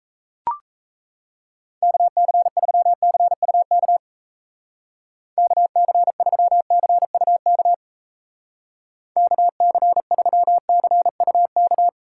Morse code, heard as a keyed tone, about 34 wpm, 700 Hz. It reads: KC3CUK KC3CUK KC3CUK
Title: Speed Racing - USA 2x3 Call Signs 28wpm